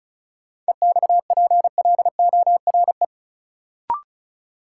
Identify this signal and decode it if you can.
Morse code, heard as a keyed tone, about 36 wpm, 700 Hz. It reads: EXPLORE
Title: Sentences from Top 1100 Words - New Word Drill Alt - 35wpm